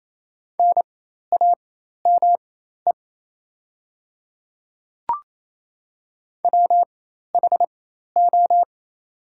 Morse code, heard as a keyed tone, about 28 wpm, spaced as for 13 wpm, 700 Hz. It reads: NAME WHO
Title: 4x Character Spacing - Top 200 Words - 28wpm